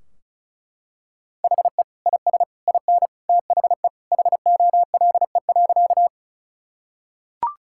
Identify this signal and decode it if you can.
Morse code, heard as a keyed tone, about 36 wpm, 700 Hz. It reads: HE IS IN THE HOLE.
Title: Sentences from Top 1600 Words 35wpm